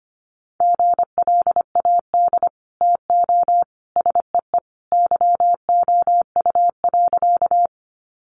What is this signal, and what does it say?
Morse code, heard as a keyed tone, about 25 wpm, 700 Hz. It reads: GLAD TO SEE YOU.
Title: Sentences from Top 800 Words 25wpm